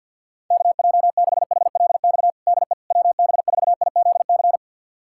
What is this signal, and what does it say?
Morse code, heard as a keyed tone, about 51 wpm, 700 Hz. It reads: KJ6HLX DE WB4IZC
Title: USA Callsign DE Callsign - 6 Characters - 50wpm